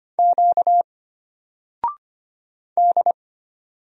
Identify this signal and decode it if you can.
Morse code, heard as a keyed tone, about 25 wpm, 700 Hz. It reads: Q D
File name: Single-Letter-Number-25wpm